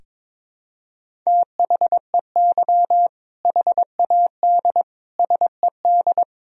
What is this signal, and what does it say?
Morse code, heard as a keyed tone, about 21 wpm, 700 Hz. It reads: THEY HAD SED
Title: Sentences from Top 400 Words and Abbreviations 22wpm